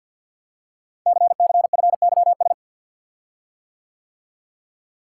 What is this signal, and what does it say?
Morse code, heard as a keyed tone, about 51 wpm, 700 Hz. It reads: CCFYS
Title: Code Groups - 5 Letters - 50wpm